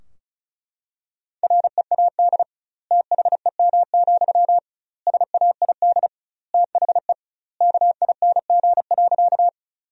Morse code, heard as a keyed tone, about 36 wpm, 700 Hz. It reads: READ THEM, SAID THE KING.
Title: 2x Word-Spacing - Sentences from Top 400 Words - 35wpm